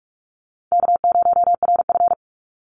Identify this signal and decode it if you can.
Morse code, heard as a keyed tone, about 45 wpm, 700 Hz. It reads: K0RF
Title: Easy Call Signs Repeated 3x - 45wpm